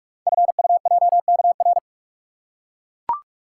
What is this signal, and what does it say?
Morse code, heard as a keyed tone, about 44 wpm, 690 Hz. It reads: FUJKR
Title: Code Groups - 5 Letters - 45wpm